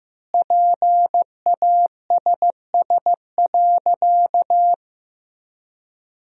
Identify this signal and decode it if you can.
Morse code, heard as a keyed tone, about 14 wpm, 700 Hz. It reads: PASS.
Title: 4x Word-Spacing - Sentences from Top 400 Words - 15wpm